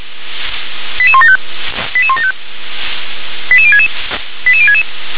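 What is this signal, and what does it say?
Five-tone selective call (ZVEI), heard as quick runs of five tones, tones of about 70 ms. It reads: 88175 88175 88066 88066
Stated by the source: ZVEI five-tone selective call, received on SDR.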